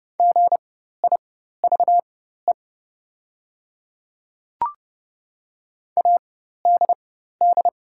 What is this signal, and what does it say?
Morse code, heard as a keyed tone, about 31 wpm, spaced as for 14 wpm, 700 Hz. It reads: GIVE ADD
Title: 4x Character Spacing - Top 300 Words - 30wpm